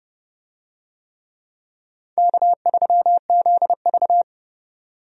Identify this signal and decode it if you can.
Morse code, heard as a keyed tone, about 30 wpm, 700 Hz. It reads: K3ZV
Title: USA Call Signs - 4 Characters - 30wpm